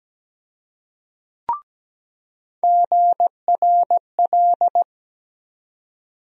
Morse code, heard as a keyed tone, about 17 wpm, 700 Hz. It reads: GRL
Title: USA Call Sign Postfixes - 3 Letters 17wpm